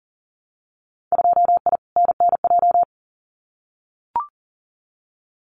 Morse code, heard as a keyed tone, about 40 wpm, 700 Hz. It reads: 2I NNJ
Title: Winter Field Day Exchanges Encore 40wpm